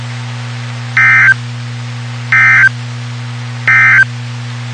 Emergency Alert System (SAME) tones, heard as harsh buzzing bursts. It is heard from 0.9 to 4.1 s.
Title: WXK91 NOAA Weather Radio